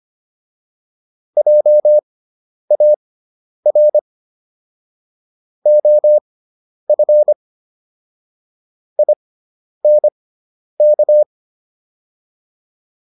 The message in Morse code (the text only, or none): JAR OF INK